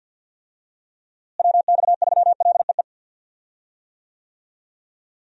Morse code, heard as a keyed tone, about 52 wpm, 700 Hz. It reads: WX3LEE
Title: USA Call Signs - 6 Characters - Repeated 2x - 50wpm